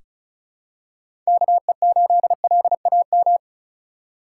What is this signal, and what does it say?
Morse code, heard as a keyed tone, about 36 wpm, 700 Hz. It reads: KE8LAM
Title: USA Call Signs - 3x Character Spacing - 35wpm